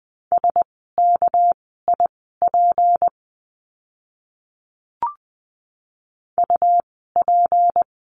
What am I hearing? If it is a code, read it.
Morse code, heard as a keyed tone, about 20 wpm, spaced as for 15 wpm, 700 Hz. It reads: SKIP UP
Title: 199 Lesson 32f - Letter k - Words 20wpm